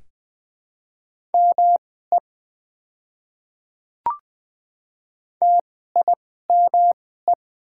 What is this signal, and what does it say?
Morse code, heard as a keyed tone, about 20 wpm, spaced as for 14 wpm, 700 Hz. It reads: ME TIME
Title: 2x Character Spacing - Top 500 Words - 20wpm